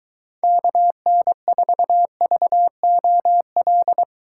Morse code, heard as a keyed tone, about 22 wpm, 700 Hz. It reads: KN4VOL